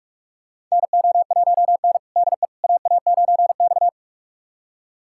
Morse code, heard as a keyed tone, about 47 wpm, 700 Hz. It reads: NO1N DE AA9X
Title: USA Callsign DE Callsign - 4 Characters - 45wpm